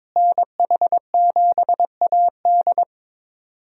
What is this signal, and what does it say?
Morse code, heard as a keyed tone, about 21 wpm, 700 Hz. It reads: NH7AD
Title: USA Call Signs - 5 Characters - Repeated 3x - 22wpm